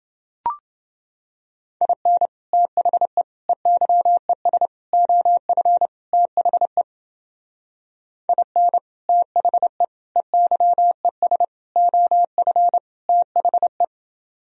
Morse code, heard as a keyed tone, about 29 wpm, 700 Hz. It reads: IN THE EYES OF THE IN THE EYES OF THE